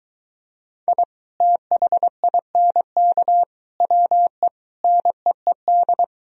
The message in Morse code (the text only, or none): I THINK WE NEED